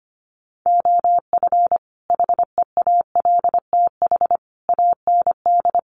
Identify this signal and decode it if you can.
Morse code, heard as a keyed tone, about 25 wpm, 700 Hz. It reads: OF HEALTH AND
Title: Sets of 5 Words 25wpm